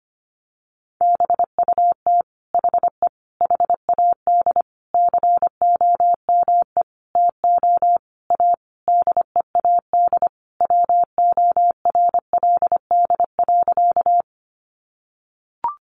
Morse code, heard as a keyed tone, about 25 wpm, 700 Hz. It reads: BUT HE HAD COME TO A DEAD WORLD.
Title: Sentences from Top 1500 Words 25wpm